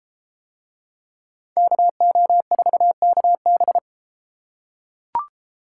Morse code, heard as a keyed tone, about 29 wpm, 700 Hz. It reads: KO4KB